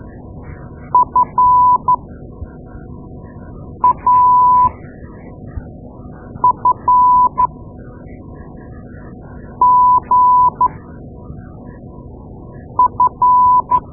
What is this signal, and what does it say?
Morse code, heard as a keyed tone, about 12 wpm, spaced as for 4 wpm, 1000 Hz. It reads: FAFGF
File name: Morse Lessons / Morse Lessons / Tape-2A